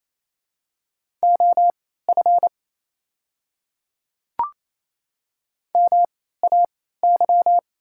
Morse code, heard as a keyed tone, about 27 wpm, spaced as for 16 wpm, 690 Hz. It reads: OF MAY
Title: 3x Character Spacing - Top 100 Words - 28wpm